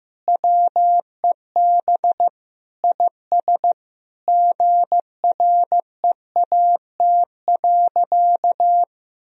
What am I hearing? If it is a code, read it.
Morse code, heard as a keyed tone, about 15 wpm, 700 Hz. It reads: WEB IS GREAT.
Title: Sentences from Top 1000 Words - Review Part 4 - 15wpm